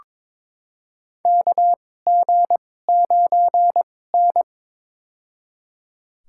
Morse code, heard as a keyed tone, about 21 wpm, spaced as for 16 wpm, 700 Hz. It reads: KG9N
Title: USA Call Signs - 4 Characters - 2x Character Spacing - 22wpm